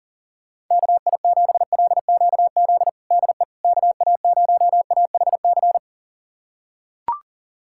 Morse code, heard as a keyed tone, about 40 wpm, 700 Hz. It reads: KI7LQZ DE KA0AHC
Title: USA Callsign DE Callsign 40wpm